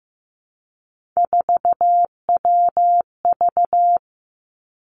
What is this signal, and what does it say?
Morse code, heard as a keyed tone, about 15 wpm, 700 Hz. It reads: 4WV
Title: Three Challenging Letter-Number Combinations 15wpm